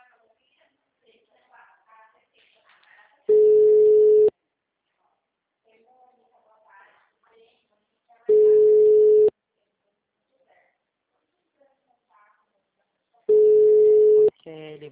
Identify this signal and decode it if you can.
Ringback tone (Europe, 425 Hz), with speech.